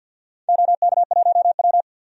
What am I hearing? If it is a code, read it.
Morse code, heard as a keyed tone, about 48 wpm, 700 Hz. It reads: OK1W
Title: DXCC-100 Call Signs - 4 Characters - 50wpm